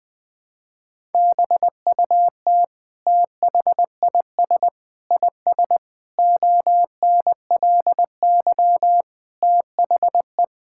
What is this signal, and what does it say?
Morse code, heard as a keyed tone, about 20 wpm, 700 Hz. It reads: BUT THIS IS ONLY THE
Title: Sentences from Top 1000 Words - Review Part 5 - 20wpm